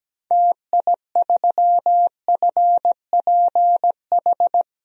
Morse code, heard as a keyed tone, about 17 wpm, 700 Hz. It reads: TI3FPH
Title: DXCC-100 Call Signs - 6 Characters - 17wpm